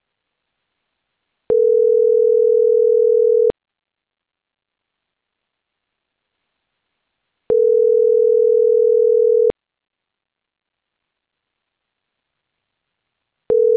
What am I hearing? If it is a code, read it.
Ringback tone (North America), 440 + 480 Hz.
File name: CENTRAL CAEA / out agcaea02 - .20993